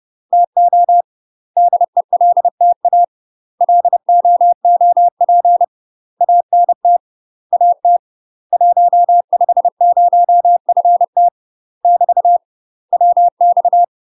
Morse code, heard as a keyed tone, about 30 wpm, 700 Hz. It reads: TO DELTA LOOP ANT AT 150FT <BT> WX
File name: UK MASTER QSOs TRAINING / 30WPM / UK MASTER QSOs 0020